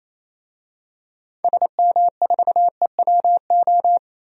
Morse code, heard as a keyed tone, about 28 wpm, 700 Hz. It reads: SM4EWO
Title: DXCC-100 Call Signs - 6 Characters - 28wpm